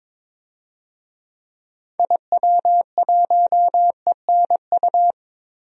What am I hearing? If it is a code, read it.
Morse code, heard as a keyed tone, about 22 wpm, 700 Hz. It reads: IW1ENU
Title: DXCC-100 Call Signs - 6 Characters - 22wpm